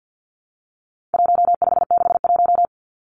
Morse code, heard as a keyed tone, about 50 wpm, 700 Hz. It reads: 1561
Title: Numbers - 4-Digit 50wpm